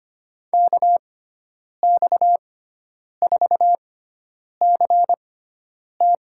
Morse code, heard as a keyed tone, about 25 wpm, spaced as for 9 wpm, 700 Hz. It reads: KX4CT
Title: USA Call Signs - 5 Characters - 6x Character Spacing - 25wpm